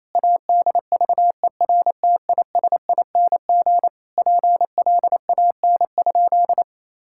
Morse code, heard as a keyed tone, about 28 wpm, 700 Hz. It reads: ADVERTISING PLAN?
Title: Sentences from Top 1000 Words - Review Part 6 - 28wpm